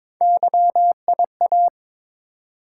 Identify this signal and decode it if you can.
Morse code, heard as a keyed tone, about 22 wpm, 700 Hz. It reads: YIA
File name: practice / 22wpm / Three Letter-Number Combinations - 22wpm